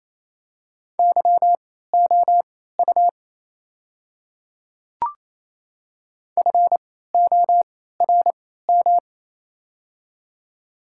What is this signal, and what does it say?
Morse code, heard as a keyed tone, about 27 wpm, spaced as for 16 wpm, 700 Hz. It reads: YOU FORM
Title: 3x Character Spacing - Top 200 Words - 28wpm